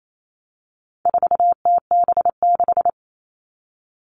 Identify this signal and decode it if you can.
Morse code, heard as a keyed tone, about 28 wpm, 700 Hz. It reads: 4TB6
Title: All Cut Numbers - 4-Digit 28wpm